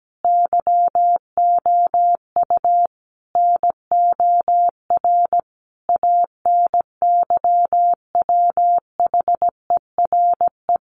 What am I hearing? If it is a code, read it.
Morse code, heard as a keyed tone, about 17 wpm, 700 Hz. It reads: YOU NOR ANYWHERE
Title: Sentences from Top 1500 Words 17wpm